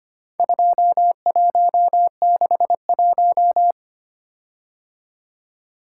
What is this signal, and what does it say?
Morse code, heard as a keyed tone, about 25 wpm, 700 Hz. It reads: 2161
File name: practice / 25wpm / Four Letter-Number Combinations - 25wpm